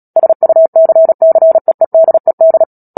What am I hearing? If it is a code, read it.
Morse code, heard as a keyed tone, about 37 wpm, 650 Hz. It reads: SUCCEEDED
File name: Book Two / 10-Chapter 10 - Epilogue - 36 WPM